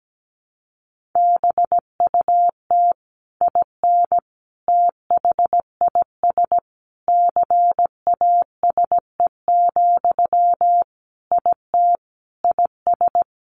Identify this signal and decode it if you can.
Morse code, heard as a keyed tone, about 17 wpm, 700 Hz. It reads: BUT IN THIS CASE, IT IS
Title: Sentences from Top 1200 Words 17wpm